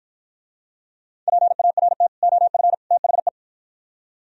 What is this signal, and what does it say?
Morse code, heard as a keyed tone, about 49 wpm, 700 Hz. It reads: PART OF THE